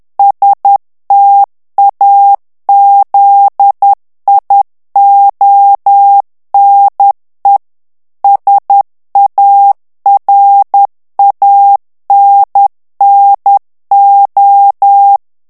Morse code, heard as a keyed tone, about 10 wpm, 800 Hz. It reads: STAZIONE SARANNO
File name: morse mp3 / ITALIEN texte clair 1 vitesse 10 mots minute